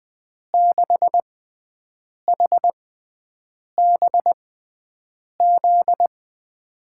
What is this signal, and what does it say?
Morse code, heard as a keyed tone, about 20 wpm, spaced as for 7 wpm, 700 Hz. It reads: 6HBZ
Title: USA Call Signs - 6 Characters - 6x Character Spacing - 20wpm